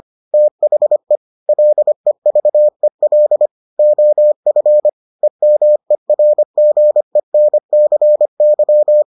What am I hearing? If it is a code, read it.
Morse code, heard as a keyed tone, about 25 wpm, 600 Hz. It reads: THE LEVEL OF EMERGENCY